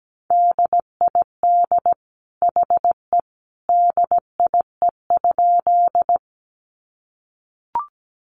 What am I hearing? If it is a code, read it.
Morse code, heard as a keyed tone, about 17 wpm, 700 Hz. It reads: DID HE DIE?
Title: Sentences from Top 1300 Words 17wpm